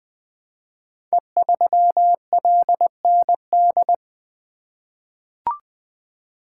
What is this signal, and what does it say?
Morse code, heard as a keyed tone, about 20 wpm, 700 Hz. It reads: E3LND
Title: Code Groups - 5 Letter:Numbers - 20wpm